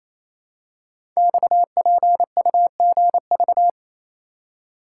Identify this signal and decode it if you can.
Morse code, heard as a keyed tone, about 27 wpm, 700 Hz. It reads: XPUGV
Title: Code Groups - 5 Letters - 28wpm